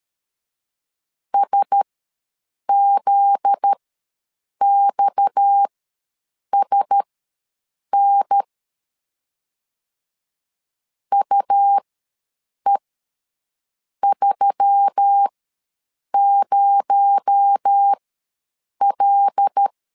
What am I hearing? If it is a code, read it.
Morse code, heard as a keyed tone, about 13 wpm, spaced as for 7 wpm, 800 Hz. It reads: SZXSN UE30L
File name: LEZIONI / 8 YJ45 / 8 Y ws 12wpm 3409